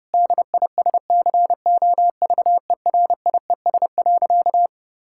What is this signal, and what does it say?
Morse code, heard as a keyed tone, about 30 wpm, 700 Hz. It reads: DISCOVERIES.
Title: Pangram Challenge v2 - 30wpm